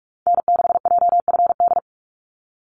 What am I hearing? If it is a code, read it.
Morse code, heard as a keyed tone, about 46 wpm, 700 Hz. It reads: N6JFD
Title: NA SOTA Activator Callsigns 45wpm